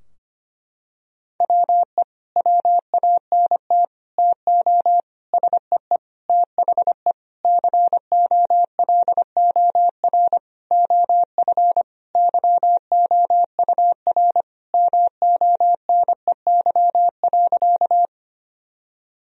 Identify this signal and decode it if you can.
Morse code, heard as a keyed tone, about 25 wpm, 700 Hz. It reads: WE WANT TO SEE THE COLOR OF YOUR MONEY.
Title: Sentences from Top 1000 Words - Review Part 2 - 25wpm